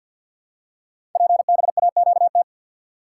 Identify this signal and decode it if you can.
Morse code, heard as a keyed tone, about 53 wpm, 690 Hz. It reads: PBAQT
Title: Code Groups - 5 Letters - 50wpm